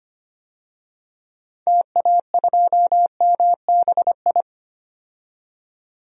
Morse code, heard as a keyed tone, about 25 wpm, 700 Hz. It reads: TA2MBI
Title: DXCC-100 Call Signs - 6 Characters - 25wpm